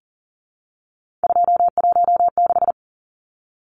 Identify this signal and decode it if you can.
Morse code, heard as a keyed tone, about 40 wpm, 700 Hz. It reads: 216
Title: Numbers - 3-Digit 40wpm